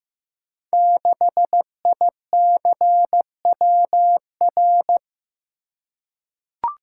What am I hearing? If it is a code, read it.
Morse code, heard as a keyed tone, about 15 wpm, 700 Hz. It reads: 6ICWR
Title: Code Groups - 5 Letter:Numbers - 15wpm